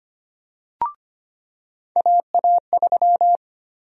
Morse code, heard as a keyed tone, about 25 wpm, 700 Hz. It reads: AA3
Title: USA Call Sign Prefixes - 25wpm